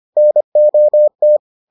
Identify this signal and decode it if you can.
Morse code, heard as a keyed tone, about 25 wpm, 600 Hz. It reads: NOT